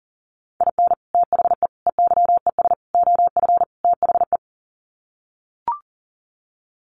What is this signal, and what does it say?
Morse code, heard as a keyed tone, about 40 wpm, 700 Hz. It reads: IN THE EYES OF THE
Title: Sets of 5 Words 40wpm